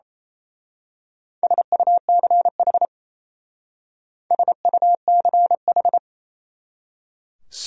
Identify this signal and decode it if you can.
Morse code, heard as a keyed tone, about 32 wpm, 700 Hz. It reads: SUCH SUCH
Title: Speed-Racing Top 200 Words 28wpm